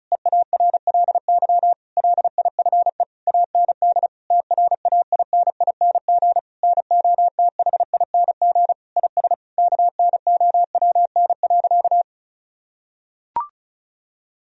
Morse code, heard as a keyed tone, about 34 wpm, 700 Hz. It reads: EARLY LIFE AND TRAINING NOTHING IS KNOWN.
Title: Sentences from Top 1000 Words - Review Part 4 - 35wpm